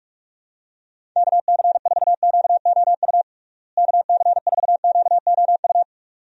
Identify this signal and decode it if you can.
Morse code, heard as a keyed tone, about 46 wpm, 700 Hz. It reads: KC4QOU KC4QOU
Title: USA Call Signs - 6 Characters - Repeated 2x - 45wpm